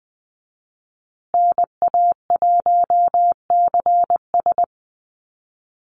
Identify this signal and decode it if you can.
Morse code, heard as a keyed tone, about 20 wpm, 700 Hz. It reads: NA1CS
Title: USA Call Signs Repeated 2x 20wpm